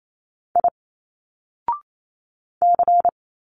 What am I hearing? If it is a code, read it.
Morse code, heard as a keyed tone, about 28 wpm, 700 Hz. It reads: I C
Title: Single Letters 28wpm